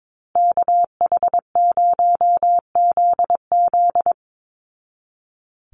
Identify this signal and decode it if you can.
Morse code, heard as a keyed tone, about 22 wpm, 700 Hz. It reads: KH0ZZ
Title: Call Signs Repeated 2x - 22wpm